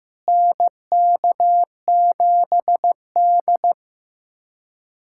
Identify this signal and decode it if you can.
Morse code, heard as a keyed tone, about 15 wpm, 700 Hz. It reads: NK7D
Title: USA Callsign DE Callsign - 4 Characters - 15wpm